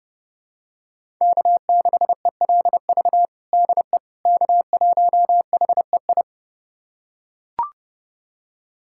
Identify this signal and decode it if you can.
Morse code, heard as a keyed tone, about 30 wpm, 700 Hz. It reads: K6ELV DE K1HEI
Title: USA Callsign DE Callsign - 5 Characters - 30wpm